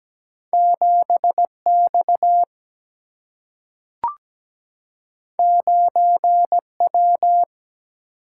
Morse code, heard as a keyed tone, about 17 wpm, 700 Hz. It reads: 7 X 9 W
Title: Single-Letter Single-Number 17wpm